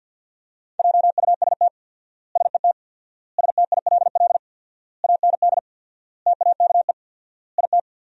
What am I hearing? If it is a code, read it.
Morse code, heard as a keyed tone, about 53 wpm, 700 Hz. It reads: JUST SET STILL AND TAKE IT
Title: 4x Word-Spacing - Sentences from Top 300 Words - 50wpm